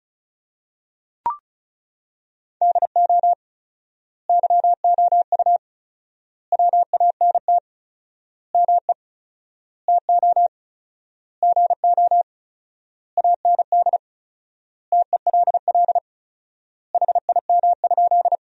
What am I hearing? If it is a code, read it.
Morse code, heard as a keyed tone, about 36 wpm, 700 Hz. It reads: DO YOU WANT ME TO GO AND TELL HIM?
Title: 4x Word-Spacing - Sentences from Top 200 Words - 35wpm